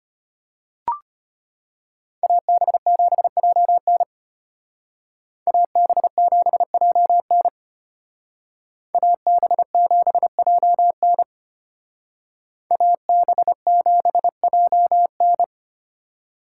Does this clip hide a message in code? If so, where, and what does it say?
Yes: Morse code from 2.2 to 15.4 s, heard as a keyed tone, about 32 wpm, 700 Hz. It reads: AB7JN AB7JN AB7JN AB7JN